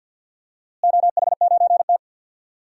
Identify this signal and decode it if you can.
Morse code, heard as a keyed tone, about 46 wpm, 700 Hz. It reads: OH9T
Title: DXCC-100 Call Signs - 4 Characters - Repeated 2x - 50wpm